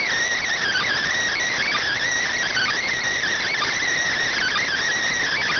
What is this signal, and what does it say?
Emergency Alert System (SAME) tones, heard as harsh buzzing bursts. It is heard from 0.8 to 4.8 s.